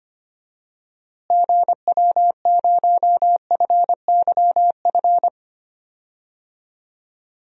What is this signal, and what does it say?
Morse code, heard as a keyed tone, about 25 wpm, 700 Hz. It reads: GW0FYF
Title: DXCC-100 Call Signs - 6 Characters - Repeated 3x - 25wpm